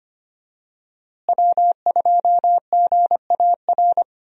Morse code, heard as a keyed tone, about 25 wpm, 700 Hz. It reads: W2GAR
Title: USA Call Signs - 5 Characters - Repeated 2x - 25wpm